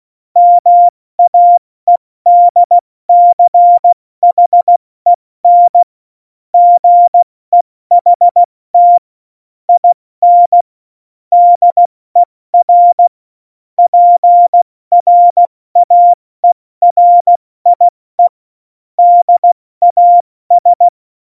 Morse code, heard as a keyed tone, about 16 wpm, 700 Hz. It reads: MAEDCHEN GEHT IN DER PRAERIE DAS
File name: Morsekurs / Witze / Witz 11 August 2020, 70 F80